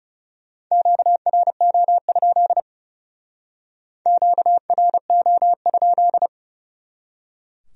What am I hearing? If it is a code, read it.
Morse code, heard as a keyed tone, about 40 wpm, 700 Hz. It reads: QRO? QRO?